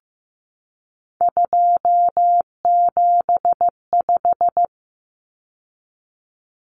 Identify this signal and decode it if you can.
Morse code, heard as a keyed tone, about 15 wpm, 700 Hz. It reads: 275
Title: Numbers - 3-Digit 15wpm